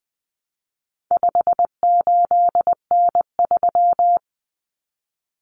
Morse code, heard as a keyed tone, about 20 wpm, 700 Hz. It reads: 58N3
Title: Common Cut Numbers - 4-Digit 20wpm